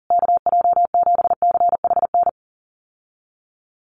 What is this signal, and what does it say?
Morse code, heard as a keyed tone, about 40 wpm, 700 Hz. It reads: KJ7CHN
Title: USA Call Signs Repeated 3x 40wpm